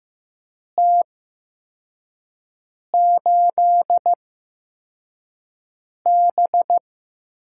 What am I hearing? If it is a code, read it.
Morse code, heard as a keyed tone, about 15 wpm, spaced as for 4 wpm, 700 Hz. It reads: T8B
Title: USA Call Signs - 4 Characters - 8x Character Spacing - 15wpm